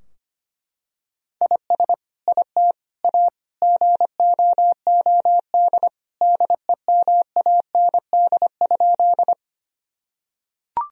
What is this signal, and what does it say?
Morse code, heard as a keyed tone, about 25 wpm, 700 Hz. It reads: IS IT A GOOD DEMAND?
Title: Sentences from Top 1000 Words - Review Part 4 - 25wpm